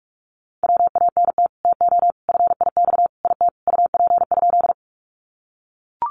Morse code, heard as a keyed tone, about 44 wpm, 700 Hz. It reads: WANT TO FIX IT UP?
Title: Sentences from Top 1400 Words 45wpm